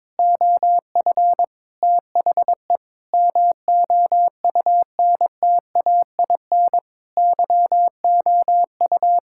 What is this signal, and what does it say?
Morse code, heard as a keyed tone, about 22 wpm, 700 Hz. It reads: OF THE MOUNTAIN YOU
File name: practice / 22wpm / Sentences with 3-Word Phrases - 22wpm